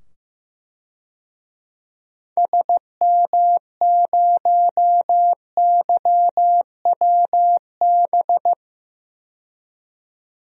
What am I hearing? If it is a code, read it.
Morse code, heard as a keyed tone, about 15 wpm, 700 Hz. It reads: SM0YWB
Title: DXCC-100 Call Signs - 6 Characters - 15wpm